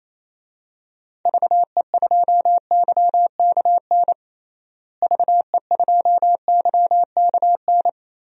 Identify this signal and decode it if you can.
Morse code, heard as a keyed tone, about 28 wpm, 700 Hz. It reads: VE2YKN VE2YKN
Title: DXCC-100 Call Signs - 6 Characters - Repeated 3x - 28wpm